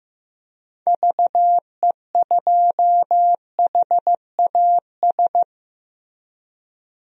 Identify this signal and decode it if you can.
Morse code, heard as a keyed tone, about 15 wpm, 700 Hz. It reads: VE2HAS
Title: DXCC-100 Call Signs - 6 Characters - Repeated 3x - 15wpm